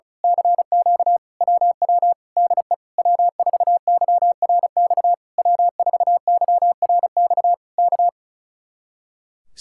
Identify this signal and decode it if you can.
Morse code, heard as a keyed tone, about 32 wpm, 700 Hz. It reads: CQ WW DE W4YRX W4YRX K
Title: Speed-Racing Common QSO Phrases Encore - 35wpm